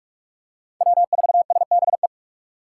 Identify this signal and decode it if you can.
Morse code, heard as a keyed tone, about 46 wpm, 700 Hz. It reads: W4SBE
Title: USA Call Signs - 6x Character Spacing - 45wpm